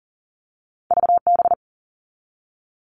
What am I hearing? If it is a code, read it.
Morse code, heard as a keyed tone, about 38 wpm, 700 Hz. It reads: VB
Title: All Cut Numbers - 2-Digit 40wpm